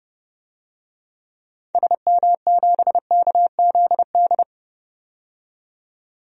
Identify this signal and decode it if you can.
Morse code, heard as a keyed tone, about 31 wpm, 700 Hz. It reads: SM7KZD
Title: DXCC-100 Call Signs - 6 Characters - 30wpm